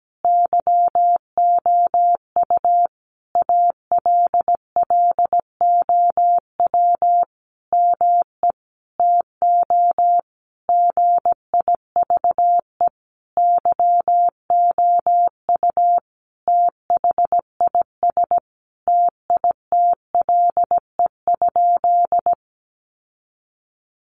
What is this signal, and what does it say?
Morse code, heard as a keyed tone, about 17 wpm, 700 Hz. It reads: YOU ALLOW ME TO GIVE YOU THIS TITLE?
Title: Sentences from Top 1200 Words 17wpm